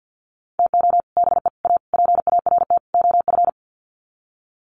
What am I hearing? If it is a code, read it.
Morse code, heard as a keyed tone, about 49 wpm, 700 Hz. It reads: TO BE A PART OF
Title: Sets of 5 Words 50wpm